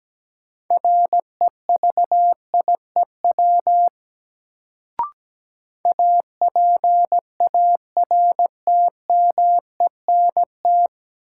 Morse code, heard as a keyed tone, about 17 wpm, 700 Hz. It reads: REVIEW APARTMENT
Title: Sentences from Top 900 Words - New Word Drill - 17wpm